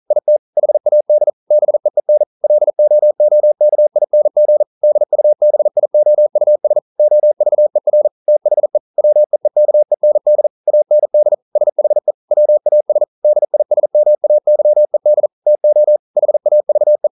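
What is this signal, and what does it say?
Morse code, heard as a keyed tone, about 40 wpm, 600 Hz. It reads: IT HAD BEEN LOOKING DUBIOUS OVER THE WEEKEND AND SHE WAS DISMAYED TO HAVE
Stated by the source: Morse code practice at 41 wpm, from CW med Gnister.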